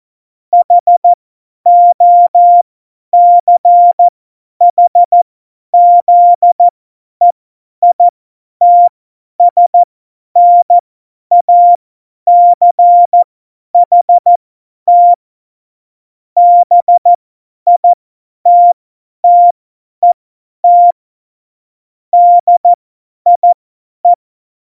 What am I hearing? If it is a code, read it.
Morse code, heard as a keyed tone, about 14 wpm, spaced as for 10 wpm, 700 Hz. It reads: HOCHZEITSNACHT BITTET DIE
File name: Morsekurs / Witze / Witz 20. August 2019, 50 F70